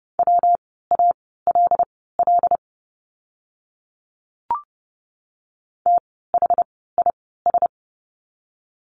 Morse code, heard as a keyed tone, about 30 wpm, spaced as for 17 wpm, 700 Hz. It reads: WALL THIS
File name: course / 30:10wpm / 115 Lesson 19f - Letter w - Words 30wpm